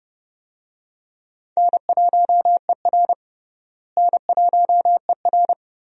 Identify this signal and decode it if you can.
Morse code, heard as a keyed tone, about 30 wpm, 700 Hz. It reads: N1ER N1ER
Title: USA Call Signs - 4 Characters - Repeated 3x - 30wpm